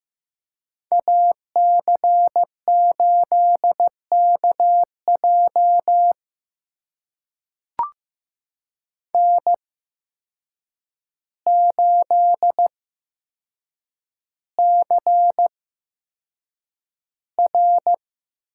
Morse code, heard as a keyed tone, about 15 wpm, spaced as for 7 wpm, 700 Hz. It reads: AC8KJ N8CR
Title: USA Call Signs - 8x Character Spacing - 15wpm